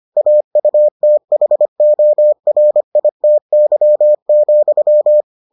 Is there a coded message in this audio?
Yes: Morse code from 0.2 to 5.2 s, heard as a keyed tone, about 25 wpm, 600 Hz. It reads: AUTHORITY,